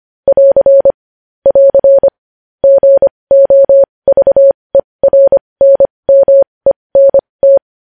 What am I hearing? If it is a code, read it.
Morse code, heard as a keyed tone, about 25 wpm, spaced as for 20 wpm, 550 Hz. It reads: <AR> <AR> GOVERNMENT
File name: podcast / News headlines-1x-25x20